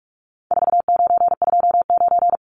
Morse code, heard as a keyed tone, about 44 wpm, 700 Hz. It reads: 4929
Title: Numbers - 4-Digit 45wpm